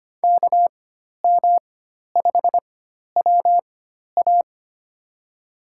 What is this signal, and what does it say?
Morse code, heard as a keyed tone, about 25 wpm, spaced as for 12 wpm, 700 Hz. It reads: KM5WA
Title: USA Call Signs - 5 Characters - 4x Character Spacing - 25wpm